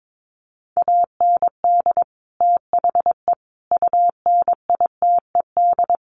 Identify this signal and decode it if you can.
Morse code, heard as a keyed tone, about 22 wpm, 700 Hz. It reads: AND THE UNITED
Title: Sets of 4 Words 22wpm